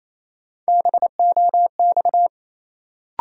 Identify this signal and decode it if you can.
Morse code, heard as a keyed tone, about 28 wpm, 700 Hz. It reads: BOX
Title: Sentences from Top 500 Words - New Word Drill - 28wpm